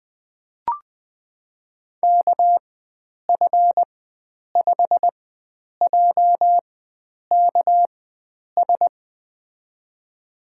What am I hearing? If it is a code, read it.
Morse code, heard as a keyed tone, about 20 wpm, spaced as for 9 wpm, 700 Hz. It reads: KF5JKS
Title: USA Call Signs - 4x Character Spacing - 20wpm